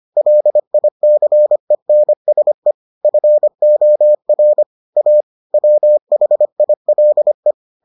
Morse code, heard as a keyed tone, about 25 wpm, 600 Hz. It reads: LICENSE FOR A WHILE